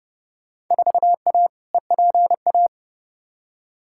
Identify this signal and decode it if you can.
Morse code, heard as a keyed tone, about 29 wpm, 700 Hz. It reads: 4A EPA
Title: Field Day Exchanges - 30wpm